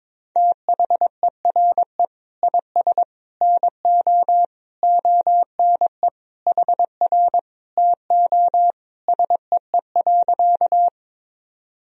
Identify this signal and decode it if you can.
Morse code, heard as a keyed tone, about 22 wpm, 700 Hz. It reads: THERE IS NO ONE HR TO SEE.
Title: Sentences from Top 200 Words and Abbreviations 22wpm